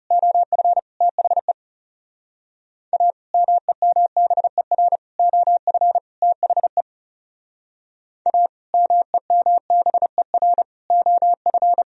Morse code, heard as a keyed tone, about 40 wpm, 700 Hz. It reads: OF THE A MEMBER OF THE A MEMBER OF